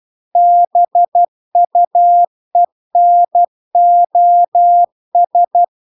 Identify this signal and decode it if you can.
Morse code, heard as a keyed tone, about 12 wpm, 700 Hz. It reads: BUENOS